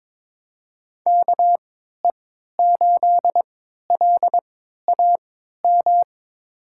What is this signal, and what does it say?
Morse code, heard as a keyed tone, about 22 wpm, spaced as for 13 wpm, 700 Hz. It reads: KE8LAM
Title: USA Call Signs - 6 Characters - 3x Character Spacing - 22wpm